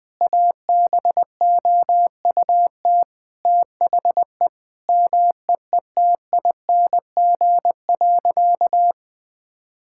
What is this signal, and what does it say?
Morse code, heard as a keyed tone, about 20 wpm, 700 Hz. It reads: ABOUT THE MEETING.
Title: Sentences with 5-Word Phrases - 20wpm